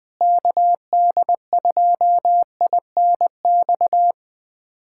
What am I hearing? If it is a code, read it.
Morse code, heard as a keyed tone, about 20 wpm, 700 Hz. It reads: KD2INX
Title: USA Callsign DE Callsign 20wpm